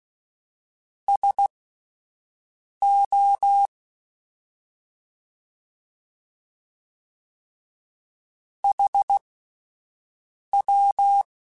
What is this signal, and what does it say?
Morse code, heard as a keyed tone, about 16 wpm, spaced as for 5 wpm, 800 Hz. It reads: SO HW
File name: (spaced) / QSOs Spaced-mp3s / WA7QOX 1 @10 sp